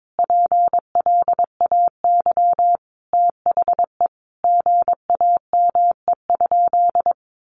Morse code, heard as a keyed tone, about 22 wpm, 700 Hz. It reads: PLAY THE GAME?
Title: Sentences from Top 1400 Words 22wpm